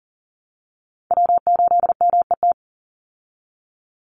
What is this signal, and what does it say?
Morse code, heard as a keyed tone, about 40 wpm, 700 Hz. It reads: W8MET
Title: Call Signs Repeated 2x - 40wpm